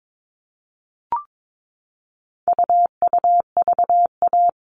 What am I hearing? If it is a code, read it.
Morse code, heard as a keyed tone, about 22 wpm, 700 Hz. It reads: UUVA
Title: All Cut Numbers - 4-Digit 22wpm